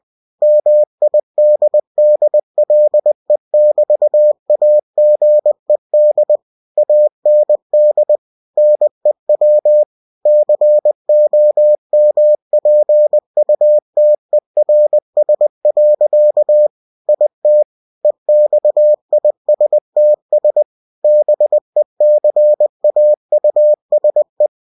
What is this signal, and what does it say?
Morse code, heard as a keyed tone, about 20 wpm, 600 Hz. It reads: MIDDLE<BT>AGED AND NEW COMPUTERS. IT EXISTS BECAUSE
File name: cw-books / winnie-the-pooh / winnie-20wpm-page-0120